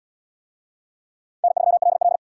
Morse code, heard as a keyed tone, about 70 wpm, 700 Hz. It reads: N3LL